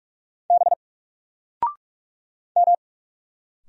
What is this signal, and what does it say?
Morse code, heard as a keyed tone, about 46 wpm, 700 Hz. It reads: B M
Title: Single Letter-Number 45wpm